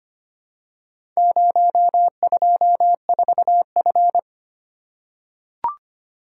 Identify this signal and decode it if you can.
Morse code, heard as a keyed tone, about 25 wpm, 700 Hz. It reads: 024F
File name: practice / 25wpm / Four Letter-Number Combinations - 25wpm